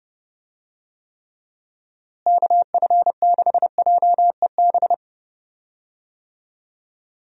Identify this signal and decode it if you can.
Morse code, heard as a keyed tone, about 30 wpm, 700 Hz. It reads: KF6JEB
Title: USA Call Signs - 6 Characters - Repeated 2x - 30wpm